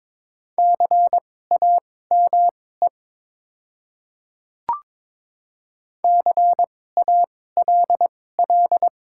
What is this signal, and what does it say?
Morse code, heard as a keyed tone, about 21 wpm, spaced as for 16 wpm, 700 Hz. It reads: CAME CALL
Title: 2x Character Spacing - Top 300 Words - 22wpm